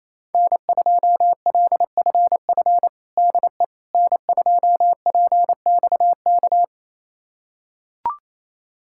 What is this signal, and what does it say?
Morse code, heard as a keyed tone, about 27 wpm, 700 Hz. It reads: N2LFF DE N2PXK
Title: USA Callsign DE Callsign - 5 Characters - 28wpm